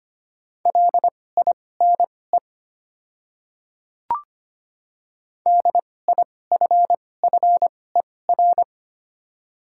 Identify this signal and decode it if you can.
Morse code, heard as a keyed tone, about 25 wpm, spaced as for 18 wpm, 700 Hz. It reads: LINE DIFFER
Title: 2x Character Spacing - Top 200 Words - 25wpm